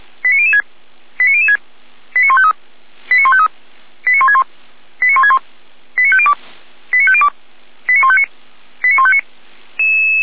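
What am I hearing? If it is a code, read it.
Five-tone selective call (ZVEI), heard as quick runs of five tones, tones of about 70 ms. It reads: 89006 89006 89153 89153 89161 89161 89502 89502 89159 89159
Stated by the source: ZVEI five-tone call received via SDR, archived by Feuerwehr Orlamünde fire brigade.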